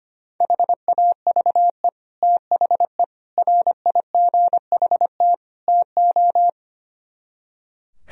Morse code, heard as a keyed tone, about 25 wpm, 700 Hz. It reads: HAVE THE RIGHT TO
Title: Speed-Racing - Sets of 4 Words 25wpm